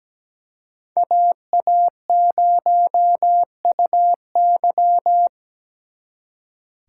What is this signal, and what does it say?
Morse code, heard as a keyed tone, about 20 wpm, 700 Hz. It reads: AA0UY
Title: Speed Racing - USA 2x2 Call Signs 17wpm